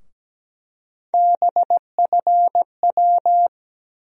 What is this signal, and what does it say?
Morse code, heard as a keyed tone, about 17 wpm, 700 Hz. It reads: BFW
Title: USA Call Sign Postfixes - 3 Letters 17wpm